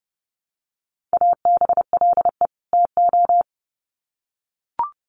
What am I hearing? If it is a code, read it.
Morse code, heard as a keyed tone, about 30 wpm, 700 Hz. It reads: ABLE TO
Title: Sets of 2 Words 30wpm